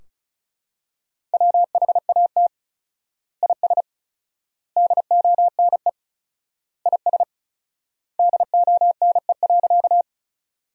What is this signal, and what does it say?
Morse code, heard as a keyed tone, about 34 wpm, 700 Hz. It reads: WHAT IS DONE IS DONE.
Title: 4x Word-Spacing - Sentences from Top 500 Words - 35wpm